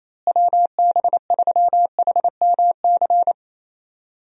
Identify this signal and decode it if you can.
Morse code, heard as a keyed tone, about 28 wpm, 700 Hz. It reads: WB3HMC
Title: USA Call Signs - 6 Characters - Repeated 2x - 28wpm